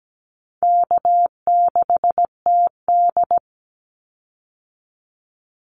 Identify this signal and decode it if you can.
Morse code, heard as a keyed tone, about 17 wpm, 700 Hz. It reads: K6TD
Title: Call Signs Repeated 3x - 17wpm